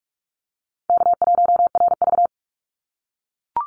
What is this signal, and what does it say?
Morse code, heard as a keyed tone, about 44 wpm, 700 Hz. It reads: K1RV
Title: Easy Call Signs - 45wpm